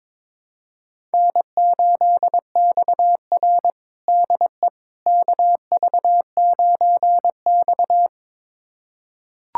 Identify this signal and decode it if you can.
Morse code, heard as a keyed tone, about 22 wpm, 700 Hz. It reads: N8XR DE KV9X
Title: USA Callsign DE Callsign - 4 Characters - 22wpm